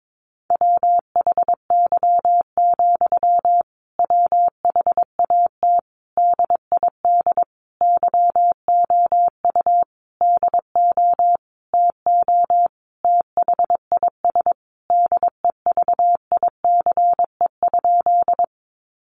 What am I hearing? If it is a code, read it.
Morse code, heard as a keyed tone, about 22 wpm, 700 Hz. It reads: WHY, WHAT DID YOU DO TO THIS DEVICE?
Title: Sentences from Top 1500 Words 22wpm